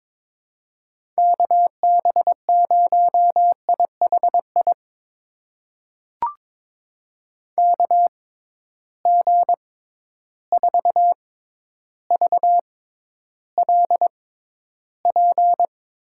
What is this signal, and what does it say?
Morse code, heard as a keyed tone, about 22 wpm, spaced as for 11 wpm, 700 Hz. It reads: KB0IHI KG4VLP
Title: USA Call Signs - 6 Characters - 6x Character Spacing - 22wpm